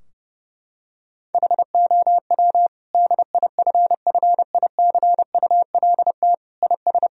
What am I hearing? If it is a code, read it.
Morse code, heard as a keyed tone, about 30 wpm, 700 Hz. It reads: HOW DIFFICULT IS
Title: Sentences from Top 1000 Words - Review Part 3 - 30wpm